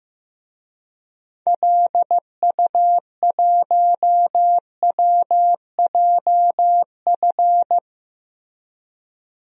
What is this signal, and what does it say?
Morse code, heard as a keyed tone, about 15 wpm, 700 Hz. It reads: LU1WJF
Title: DXCC-100 Call Signs - 6 Characters - Repeated 2x - 15wpm